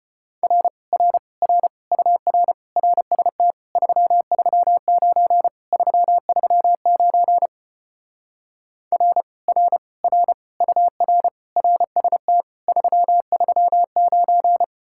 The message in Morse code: R R R UR RST 339 339 R R R UR RST 339

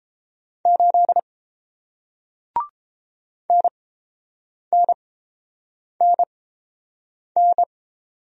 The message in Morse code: 8 NNNN

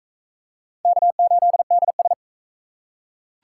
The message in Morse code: K8DS

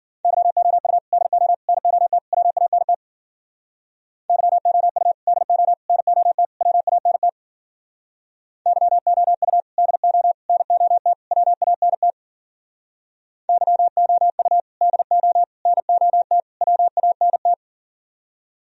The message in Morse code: YOU DO NOT WANT YOU DO NOT WANT YOU DO NOT WANT YOU DO NOT WANT

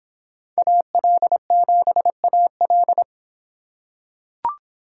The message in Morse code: AL7AL

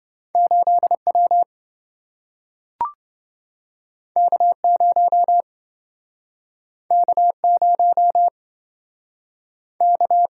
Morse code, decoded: 8W K0 K0 K

